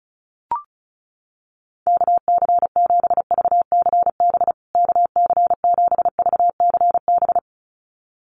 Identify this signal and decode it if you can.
Morse code, heard as a keyed tone, about 34 wpm, 700 Hz. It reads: KC7VCB KC7VCB
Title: USA Call Signs Repeated 2x 35wpm